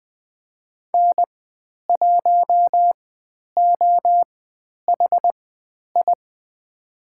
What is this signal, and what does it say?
Morse code, heard as a keyed tone, about 20 wpm, spaced as for 10 wpm, 700 Hz. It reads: N1OHI
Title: 095 Lesson 16g - Letter c - Call Signs 20wpm